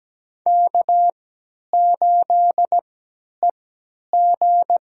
Morse code, heard as a keyed tone, about 17 wpm, spaced as for 10 wpm, 700 Hz. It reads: K8EG
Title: USA Call Signs - 4 Characters - 3x Character Spacing - 17wpm